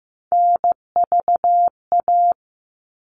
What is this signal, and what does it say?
Morse code, heard as a keyed tone, about 15 wpm, 700 Hz. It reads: NVA
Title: All Cut Numbers - 3-Digit 15wpm